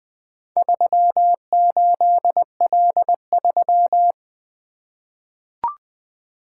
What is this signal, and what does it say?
Morse code, heard as a keyed tone, about 20 wpm, 700 Hz. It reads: 38L3
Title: Four Letter-Number Combinations - 20wpm